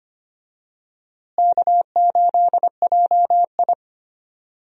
Morse code, heard as a keyed tone, about 25 wpm, 700 Hz. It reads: K8JI
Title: USA Call Signs - 4 Characters - Repeated 3x - 25wpm